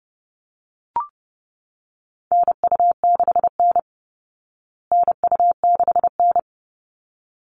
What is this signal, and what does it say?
Morse code, heard as a keyed tone, about 30 wpm, 700 Hz. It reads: NU6N NU6N
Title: Call Signs Repeated 3x - 30wpm